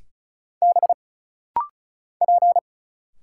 Morse code, heard as a keyed tone, about 34 wpm, 700 Hz. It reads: B P
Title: Single Letters - Rapid-Fire 35wpm